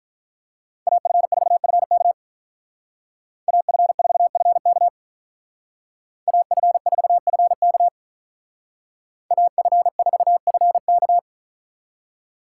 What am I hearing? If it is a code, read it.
Morse code, heard as a keyed tone, about 43 wpm, 700 Hz. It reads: AF4FK AF4FK AF4FK AF4FK
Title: Speed Racing - USA 2x2 Call Signs 35wpm